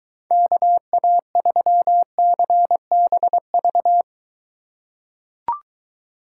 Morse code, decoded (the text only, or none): KA3CBV